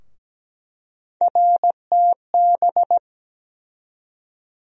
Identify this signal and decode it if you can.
Morse code, heard as a keyed tone, about 17 wpm, 700 Hz. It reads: RTB
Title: USA Call Sign Postfixes - 3 Letters 17wpm